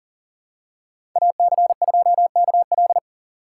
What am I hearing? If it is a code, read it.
Morse code, heard as a keyed tone, about 40 wpm, 700 Hz. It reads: AC2KL
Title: USA Call Signs - 4x Character Spacing - 40wpm